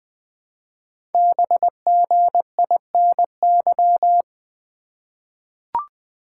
Morse code, heard as a keyed tone, about 20 wpm, 700 Hz. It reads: BGINY
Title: Code Groups - 5 Letters - 20wpm